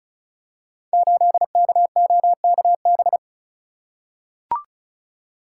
Morse code, heard as a keyed tone, about 36 wpm, 700 Hz. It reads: 8KOKB
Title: Code Groups - 5 Letter:Numbers - 35wpm